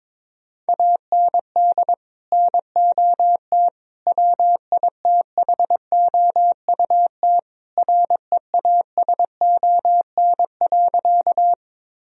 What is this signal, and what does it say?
Morse code, heard as a keyed tone, about 21 wpm, 700 Hz. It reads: AND NOT WITHOUT REASON.
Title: Sentences from Top 1000 Words - Review Part 3 - 22wpm